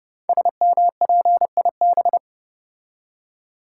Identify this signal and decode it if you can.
Morse code, heard as a keyed tone, about 30 wpm, 700 Hz. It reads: SMPIB
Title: Code Groups - 5 Letters - 30wpm